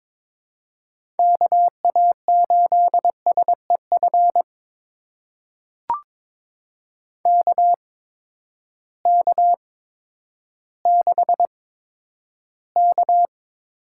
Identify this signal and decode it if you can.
Morse code, heard as a keyed tone, about 23 wpm, spaced as for 11 wpm, 700 Hz. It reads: KA8SEF KK6K
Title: USA Call Signs - 6 Characters - 8x Character Spacing - 22wpm